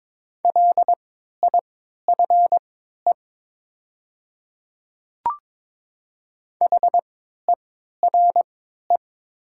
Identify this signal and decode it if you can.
Morse code, heard as a keyed tone, about 22 wpm, spaced as for 12 wpm, 700 Hz. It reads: LIFE HERE
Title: 3x Character Spacing - Top 300 Words - 22wpm